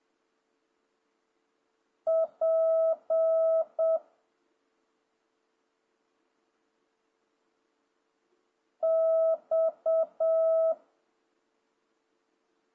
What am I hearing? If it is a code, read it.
Morse code, heard as a keyed tone, about 7 wpm, 1310 Hz. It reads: P X
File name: jh8cbh / 2026cw